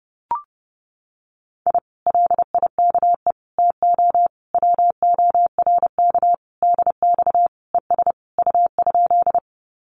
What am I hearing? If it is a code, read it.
Morse code, heard as a keyed tone, about 30 wpm, 700 Hz. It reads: I LIKE TO WORK DX ES U?
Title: Common QSO Phrases Encore 30wpm